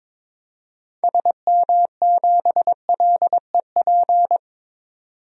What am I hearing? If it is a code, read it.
Morse code, heard as a keyed tone, about 22 wpm, 700 Hz. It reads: SM7LEP
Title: DXCC-100 Call Signs - 6 Characters - 22wpm